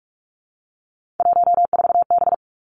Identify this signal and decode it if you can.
Morse code, heard as a keyed tone, about 46 wpm, 700 Hz. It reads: 14B
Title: Three Challenging Letter-Number Combinations 45wpm